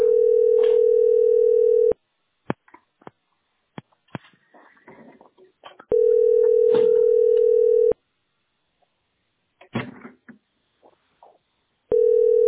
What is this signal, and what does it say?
Ringback tone (North America), 440 + 480 Hz.